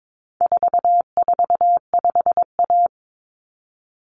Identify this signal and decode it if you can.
Morse code, heard as a keyed tone, about 22 wpm, 700 Hz. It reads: 445A
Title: Common Cut Numbers - 4-Digit 22wpm